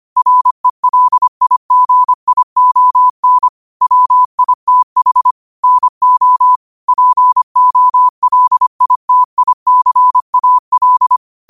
Morse code, heard as a keyed tone, about 25 wpm, 1000 Hz. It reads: RELIGION WITH NO POLITICAL
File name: 25wpm